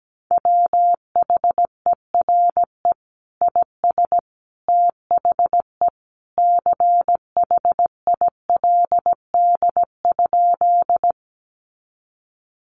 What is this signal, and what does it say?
Morse code, heard as a keyed tone, about 17 wpm, 700 Hz. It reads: WHERE IS THE CHILD?
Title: Sentences from Top 800 Words 17wpm